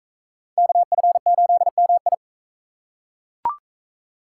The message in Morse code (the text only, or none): KF8MI